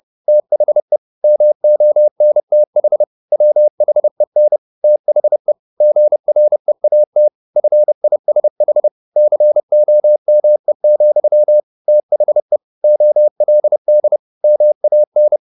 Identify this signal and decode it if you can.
Morse code, heard as a keyed tone, about 30 wpm, 600 Hz. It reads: THE MONTH WHEN THE GREAT FISH COME, THE OLD MAN
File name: the old man and the sea / the old man and the sea 1 30wpm